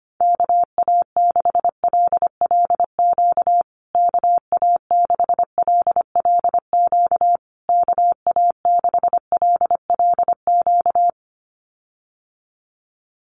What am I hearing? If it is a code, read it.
Morse code, heard as a keyed tone, about 25 wpm, 700 Hz. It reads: KA6LLQ KA6LLQ KA6LLQ
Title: USA Call Signs Repeated 3x 25wpm